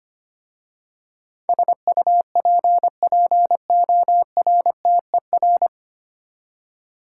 Morse code, heard as a keyed tone, about 25 wpm, 690 Hz. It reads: SUPPORTER
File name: practice / 25wpm / 9 Letter Words - First 3 Letters Given - 25wpm